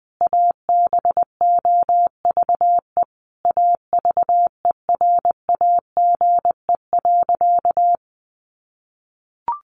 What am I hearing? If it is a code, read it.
Morse code, heard as a keyed tone, about 20 wpm, 700 Hz. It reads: ABOVE AVERAGE.
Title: Sentences from Top 1400 Words 20wpm